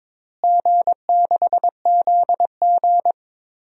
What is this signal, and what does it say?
Morse code, heard as a keyed tone, about 22 wpm, 700 Hz. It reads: G6ZG
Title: DXCC-100 Call Signs - 4 Characters - Repeated 3x - 22wpm